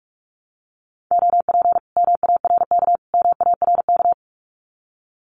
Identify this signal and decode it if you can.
Morse code, heard as a keyed tone, about 46 wpm, 700 Hz. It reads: OP MARK MARK
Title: Common QSO Phrases Encore 45wpm